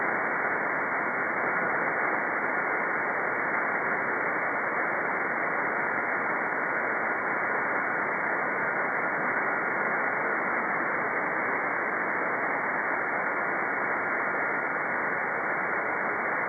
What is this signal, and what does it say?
FT8.
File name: contests / 2019-june=vhf-k5tr- -0101 0-2